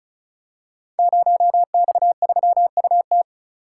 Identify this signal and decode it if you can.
Morse code, heard as a keyed tone, about 34 wpm, 700 Hz. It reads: 0X3UT
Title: Code Groups - 5 Letter:Numbers - 35wpm